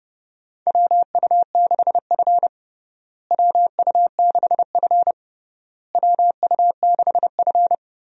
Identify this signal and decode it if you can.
Morse code, heard as a keyed tone, about 31 wpm, 700 Hz. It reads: WU6F WU6F WU6F
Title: USA Call Signs - 4 Characters - Repeated 3x - 30wpm